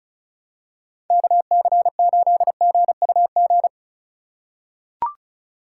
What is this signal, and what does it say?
Morse code, heard as a keyed tone, about 34 wpm, 700 Hz. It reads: KC8GUG
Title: USA Call Signs - 3x Character Spacing - 35wpm